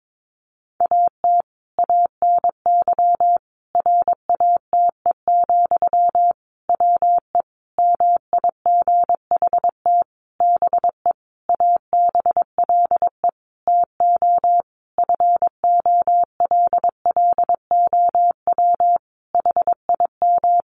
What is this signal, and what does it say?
Morse code, heard as a keyed tone, about 22 wpm, 700 Hz. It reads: AT ANY RATE, WE MIGHT BE ABLE TO FOLLOW HIM
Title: Sentences from Top 700 Words 22wpm